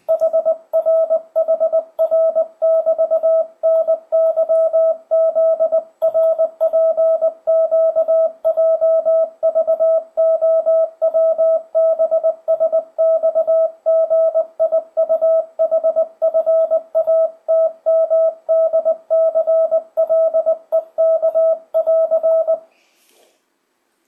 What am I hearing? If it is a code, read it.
Morse code, heard as a keyed tone, about 20 wpm, 650 Hz, with radio noise. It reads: HR HR<BT>NYZRPQJVOWBSXGIUHFATMDCLEK<AR>
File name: 2025cw / speedup / 20wpm